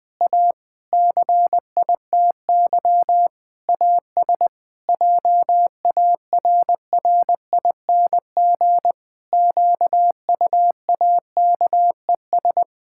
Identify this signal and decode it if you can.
Morse code, heard as a keyed tone, about 21 wpm, 700 Hz. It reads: A CITY AS JARRING QUAKES
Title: Pangram Challenge v2 - 20wpm